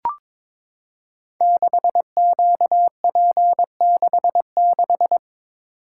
Morse code, heard as a keyed tone, about 22 wpm, 700 Hz. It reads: 6QP66
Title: Code Groups - 5 Letter:Numbers - 22wpm